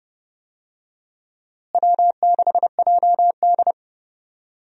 Morse code, heard as a keyed tone, about 32 wpm, 700 Hz. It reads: W6JD
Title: USA Call Signs - 4 Characters - Repeated 3x - 30wpm